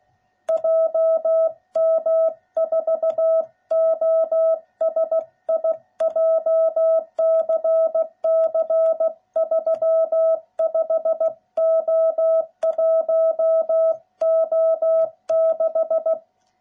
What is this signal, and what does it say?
Morse code, heard as a keyed tone, about 16 wpm, 660 Hz. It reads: JM4OSI JCC 35O1O6